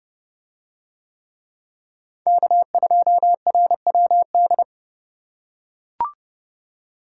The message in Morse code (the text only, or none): K2RWD